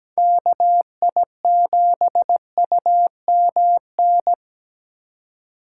Morse code, heard as a keyed tone, about 20 wpm, 700 Hz. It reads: KI7UMN
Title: Speed Racing - USA 2x3 Call Signs 17wpm